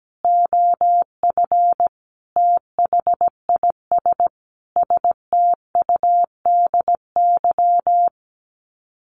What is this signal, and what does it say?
Morse code, heard as a keyed tone, about 17 wpm, 700 Hz. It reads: OF THIS STUDY
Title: Sets of 5 Words Encore2 17wpm